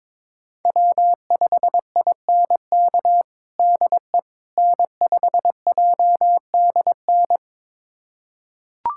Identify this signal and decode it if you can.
Morse code, heard as a keyed tone, about 22 wpm, 700 Hz. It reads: W5INK DE N5JDN
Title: USA Callsign DE Callsign - 5 Characters - 22wpm